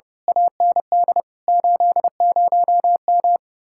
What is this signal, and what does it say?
Morse code, heard as a keyed tone, about 34 wpm, 700 Hz. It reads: AND 80M
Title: Speed-Racing Common QSO Phrases - 30wpm